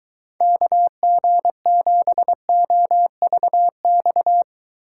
Morse code, heard as a keyed tone, about 23 wpm, 700 Hz. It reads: KG7OVX